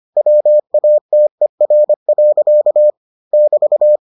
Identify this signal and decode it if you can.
Morse code, heard as a keyed tone, about 25 wpm, 600 Hz. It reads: WATER. <BT>